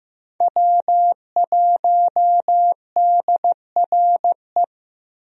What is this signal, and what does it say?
Morse code, heard as a keyed tone, about 16 wpm, 700 Hz. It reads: W1DRE
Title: USA Call Signs - 5 Characters - Repeated 2x - 15wpm